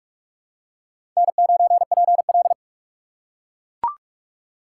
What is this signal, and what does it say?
Morse code, heard as a keyed tone, about 41 wpm, 700 Hz. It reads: N9PL
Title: USA Call Signs - 4 Characters - 2x Character Spacing - 45wpm